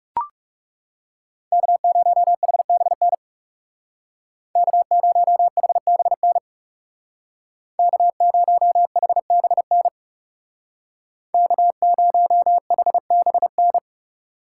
Morse code, heard as a keyed tone, about 36 wpm, 700 Hz. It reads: K0HBN K0HBN K0HBN K0HBN
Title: Speed Racing - USA 1x3 Call Signs 30wpm